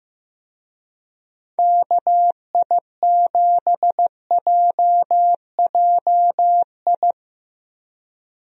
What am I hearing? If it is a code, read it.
Morse code, heard as a keyed tone, about 15 wpm, 700 Hz. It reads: KI7JJI
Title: USA Call Signs - 6 Characters - 15wpm